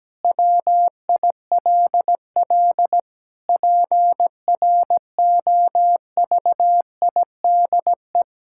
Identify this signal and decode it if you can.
Morse code, heard as a keyed tone, about 17 wpm, 700 Hz. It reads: WILL PROVIDE
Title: Sentences from Top 1000 Words - Review Part 6 - 17wpm